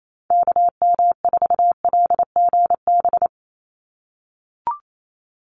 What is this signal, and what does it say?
Morse code, heard as a keyed tone, about 28 wpm, 700 Hz. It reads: KM4LGB
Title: USA Call Signs 28wpm